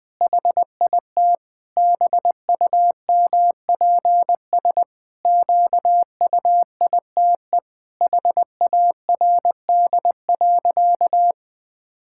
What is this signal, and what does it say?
Morse code, heard as a keyed tone, about 20 wpm, 700 Hz. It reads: HIT BUMPS QUITE HARD.
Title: Pangram Challenge v2 - 20wpm